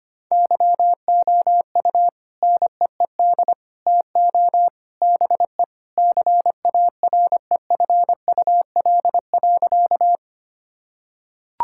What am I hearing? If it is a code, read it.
Morse code, heard as a keyed tone, about 25 wpm, 700 Hz. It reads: YOU NEED TO BE CAREFUL.
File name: practice / 25wpm / Sentences with 3-Word Phrases - 25wpm